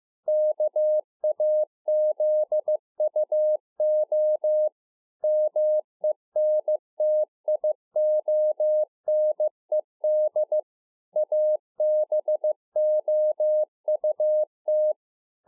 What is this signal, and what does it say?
Morse code, heard as a keyed tone, about 15 wpm, 610 Hz. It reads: KAZUO MENTIONED ABOUT
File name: 2011-04-mp3 / THE 3.11 6 15WPM